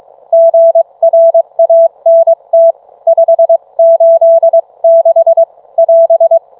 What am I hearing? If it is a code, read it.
Morse code, heard as a keyed tone, about 22 wpm, 660 Hz. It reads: GRANT 586 <AS>